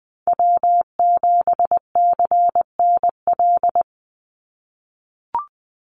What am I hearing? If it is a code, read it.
Morse code, heard as a keyed tone, about 20 wpm, 700 Hz. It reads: W7CNL
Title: NA SOTA Chaser Callsigns 20wpm